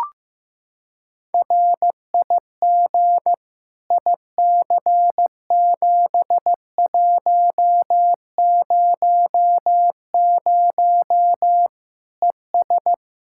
Morse code, heard as a keyed tone, about 15 wpm, 700 Hz. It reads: RIG IC7100 ES
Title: K1USN SST QSO Practice 15wpm